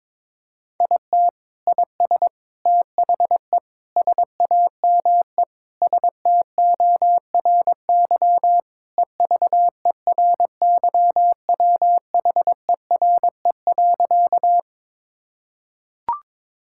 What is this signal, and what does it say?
Morse code, heard as a keyed tone, about 22 wpm, 700 Hz. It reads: IT IS THE SAME STORY EVERYWHERE.
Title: Sentences from Top 1600 Words 22wpm